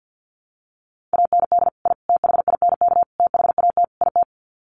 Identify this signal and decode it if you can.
Morse code, heard as a keyed tone, about 50 wpm, 700 Hz. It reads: AND I THINK THAT IT
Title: Sets of 5 Words Encore2 50wpm